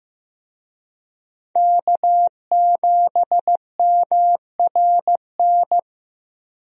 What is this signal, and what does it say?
Morse code, heard as a keyed tone, about 16 wpm, 700 Hz. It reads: K7MRN
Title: USA Call Signs - 5 Characters - Repeated 3x - 15wpm